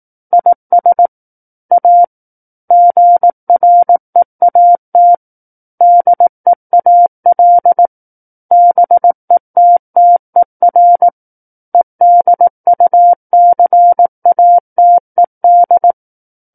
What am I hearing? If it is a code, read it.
Morse code, heard as a keyed tone, about 18 wpm, 700 Hz. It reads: IS A GREAT DEAL BETTER EDUCATED
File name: Book One / 09-Chapter 9 - 18 WPM